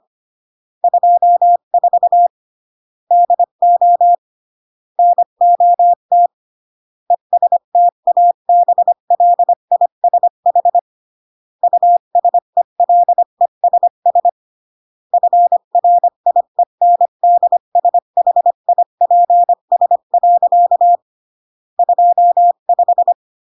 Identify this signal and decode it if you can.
Morse code, heard as a keyed tone, about 25 wpm, 700 Hz. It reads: 24 DO NOT ESTABLISH USELESS FRIENDSHIPS. 25
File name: CW / jodorowsky / 82 commandments 25-20